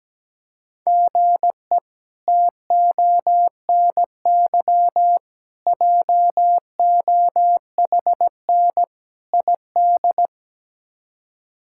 Morse code, heard as a keyed tone, about 17 wpm, 700 Hz. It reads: GE TONY JOHN ID
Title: K1USN SST QSO Practice - 17wpm